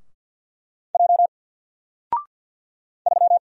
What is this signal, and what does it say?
Morse code, heard as a keyed tone, about 50 wpm, 700 Hz. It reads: J 3
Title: Single Letter-Number 50wpm